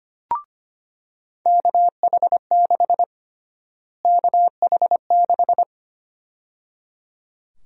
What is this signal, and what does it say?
Morse code, heard as a keyed tone, about 25 wpm, 700 Hz. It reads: KH6 KH6
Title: USA Call Sign Prefixes - Repeated 2x - 25wpm